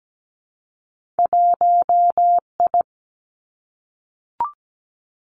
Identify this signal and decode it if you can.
Morse code, heard as a keyed tone, about 17 wpm, 700 Hz. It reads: 1I
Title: Two Challenging Letter-Number Combinations 17wpm